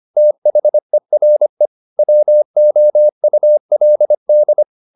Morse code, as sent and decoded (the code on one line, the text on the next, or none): - .... .-. / .-- --- ..- .-.. -..
THERE WOULD